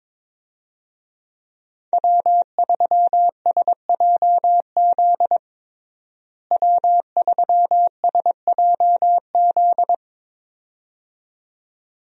Morse code, heard as a keyed tone, about 22 wpm, 700 Hz. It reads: W3SJZ W3SJZ
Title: USA Call Signs - 5 Characters - Repeated 2x - 22wpm